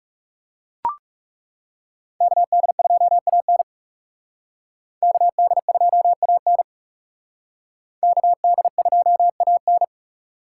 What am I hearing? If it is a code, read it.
Morse code, heard as a keyed tone, about 37 wpm, 700 Hz. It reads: KD2AN KD2AN KD2AN
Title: Speed Racing - USA 2x2 Call Signs 30wpm